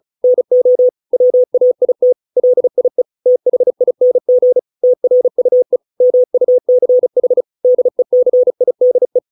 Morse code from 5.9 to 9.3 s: MUCH DECIDE